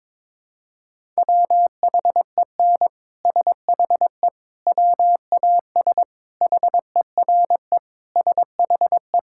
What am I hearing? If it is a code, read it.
Morse code, heard as a keyed tone, about 22 wpm, 700 Hz. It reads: WHEN SHE WAS HERE SHE
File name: practice / 22wpm / Sentences with 3-Word Phrases - 22wpm